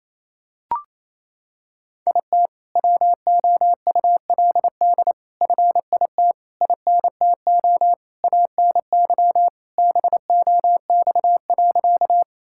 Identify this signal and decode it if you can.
Morse code, heard as a keyed tone, about 28 wpm, 700 Hz. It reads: IT WOULD FIT INTO ANY BOX.
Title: Sentences from Top 1000 Words - Review Part 5 - 28wpm